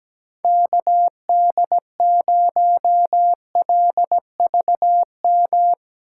Morse code, heard as a keyed tone, about 17 wpm, 700 Hz. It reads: KD0LVM
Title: USA Call Signs - 6 Characters - 3x Character Spacing - 17wpm